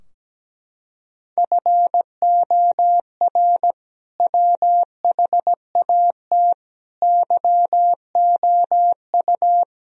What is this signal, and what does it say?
Morse code, heard as a keyed tone, about 17 wpm, 700 Hz. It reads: FOR WHAT YOU
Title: Sentences from Top 1000 Words - Review Part 2 - 17wpm